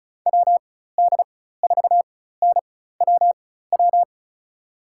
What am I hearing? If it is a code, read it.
Morse code, heard as a keyed tone, about 36 wpm, spaced as for 16 wpm, 700 Hz. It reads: WD4NWW
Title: USA Call Signs - 6 Characters - 4x Character Spacing - 35wpm